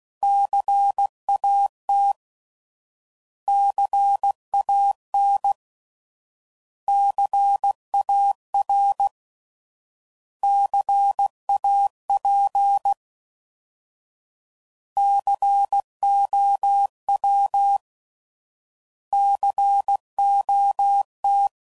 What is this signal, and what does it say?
Morse code, heard as a keyed tone, about 16 wpm, 800 Hz. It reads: CAT CAN CAR CAP COW COT
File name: Prefixes / By First Letter / Cs